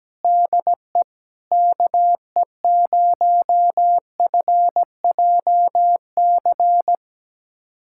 Morse code, heard as a keyed tone, about 17 wpm, 700 Hz. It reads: DE KE0FJC
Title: USA Callsign DE Callsign - 6 Characters - 17wpm